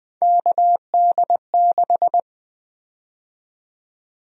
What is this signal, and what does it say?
Morse code, heard as a keyed tone, about 20 wpm, 700 Hz. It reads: KD6
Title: USA Call Sign Prefixes - Repeated 2x - 20wpm